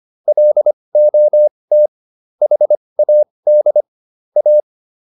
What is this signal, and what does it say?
Morse code, heard as a keyed tone, about 25 wpm, spaced as for 20 wpm, 600 Hz. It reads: LOT HAD A